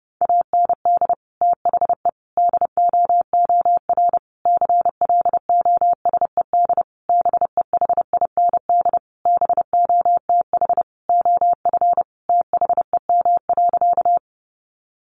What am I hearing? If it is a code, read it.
Morse code, heard as a keyed tone, about 30 wpm, 700 Hz. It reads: AND THE DOOR CLOSED BEHIND BOTH OF THEM.
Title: Sentences from Top 1400 Words 30wpm